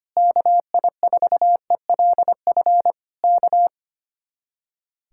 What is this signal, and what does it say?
Morse code, heard as a keyed tone, about 24 wpm, 700 Hz. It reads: KI4ELF K
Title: Speed-Racing Common QSO Phrases Encore - 25wpm